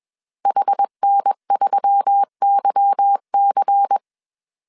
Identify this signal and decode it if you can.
Morse code, heard as a keyed tone, about 21 wpm, 800 Hz. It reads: HN3YC